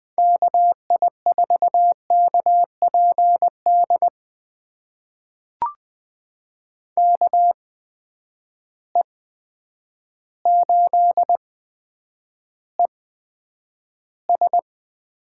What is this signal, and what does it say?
Morse code, heard as a keyed tone, about 21 wpm, spaced as for 9 wpm, 700 Hz. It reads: KI4KPD KE8ES
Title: USA Call Signs - 6 Characters - 8x Character Spacing - 20wpm